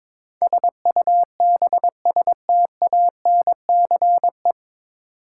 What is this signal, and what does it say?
Morse code, heard as a keyed tone, about 22 wpm, 700 Hz. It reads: SUBSTANCE
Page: Morse Code Ninja practice 22wpm